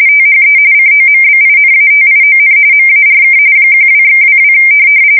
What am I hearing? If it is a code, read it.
RTTY, about 45.45 baud. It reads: THE COMBINATION MARK 12/22
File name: fsk170 / USS-Missouri